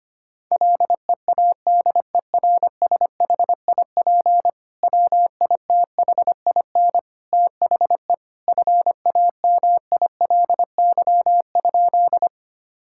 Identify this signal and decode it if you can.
Morse code, heard as a keyed tone, about 25 wpm, 700 Hz. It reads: LEADERSHIP WITHIN THE FAMILY?
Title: Sentences from Top 1600 Words 25wpm